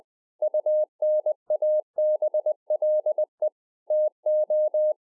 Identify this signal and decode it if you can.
Morse code, heard as a keyed tone, about 20 wpm, 610 Hz. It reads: UNABLE TO